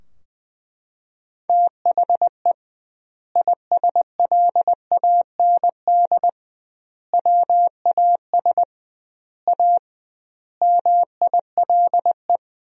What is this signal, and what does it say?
Morse code, heard as a keyed tone, about 21 wpm, 700 Hz. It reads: THE ISLAND WAS A MILE
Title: 2x Word-Spacing - Sentences from Top 500 Words - 20wpm